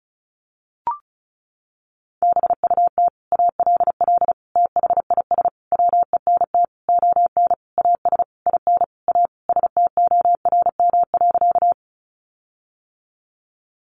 Morse code, heard as a keyed tone, about 34 wpm, 700 Hz. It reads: BUT ALL THIS WENT ON AS IN A STORM.
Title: Sentences from Top 1500 Words 35wpm